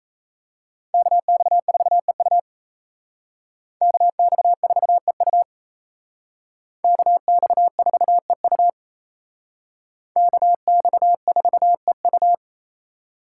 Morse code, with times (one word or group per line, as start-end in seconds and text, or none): 0.9-2.4 KX4EU
3.8-5.4 KX4EU
6.8-8.7 KX4EU
10.2-12.3 KX4EU